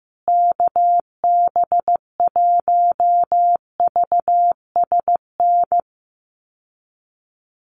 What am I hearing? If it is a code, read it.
Morse code, heard as a keyed tone, about 15 wpm, 700 Hz. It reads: KB1VSN
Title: USA Call Signs Repeated 2x 15wpm